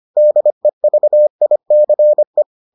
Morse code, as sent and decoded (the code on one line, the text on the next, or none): -.. ...- .. -.-. .
DEVICE